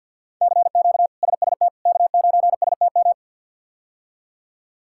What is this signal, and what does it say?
Morse code, heard as a keyed tone, about 45 wpm, 700 Hz. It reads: CQ SST K9STM
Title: K1USN SST QSO Practice - 50wpm